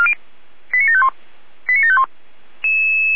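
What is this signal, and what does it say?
Five-tone selective call (ZVEI), heard as quick runs of five tones, tones of about 80 ms. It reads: 89751 89751
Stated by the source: ZVEI five-tone selective call received with an SDR.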